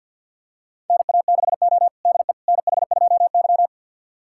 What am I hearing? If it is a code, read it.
Morse code, heard as a keyed tone, about 52 wpm, 700 Hz. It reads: NA6O DE NH2Y
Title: USA Callsign DE Callsign - 4 Characters - 50wpm